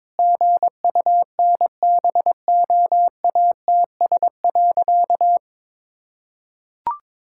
Morse code, heard as a keyed tone, about 22 wpm, 700 Hz. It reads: GUNBOATS.
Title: Pangram Challenge v2 - 22wpm